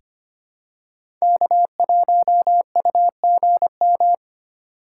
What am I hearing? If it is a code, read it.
Morse code, heard as a keyed tone, about 25 wpm, 700 Hz. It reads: K1UGM
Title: USA Call Signs - 5 Characters - 3x Character Spacing - 25wpm